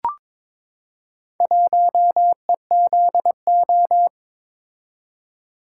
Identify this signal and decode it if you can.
Morse code, heard as a keyed tone, about 22 wpm, 690 Hz. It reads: 1EZO
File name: practice / 22wpm / Four Letter-Number Combinations - 22wpm